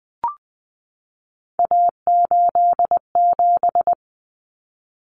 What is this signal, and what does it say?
Morse code, heard as a keyed tone, about 20 wpm, 700 Hz. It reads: A87
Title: Common Cut Numbers - 3-Digit 20wpm